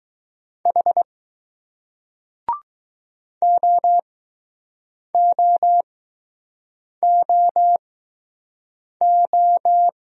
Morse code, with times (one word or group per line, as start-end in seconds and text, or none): 0.7-1.0 H
3.4-9.9 OOOO